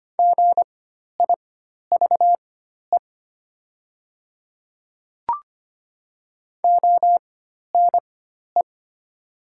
Morse code, heard as a keyed tone, about 25 wpm, spaced as for 12 wpm, 700 Hz. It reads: GIVE ONE
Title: 4x Character Spacing - Top 500 Words - 25wpm